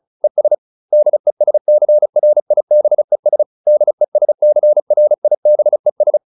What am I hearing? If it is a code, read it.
Morse code, heard as a keyed tone, about 34 wpm, 600 Hz. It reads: ES DESCRIBES DESCRIBES